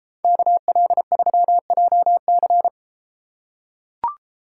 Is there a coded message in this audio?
Yes: Morse code from 0.2 to 2.7 s, heard as a keyed tone, about 32 wpm, 700 Hz. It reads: KL3JC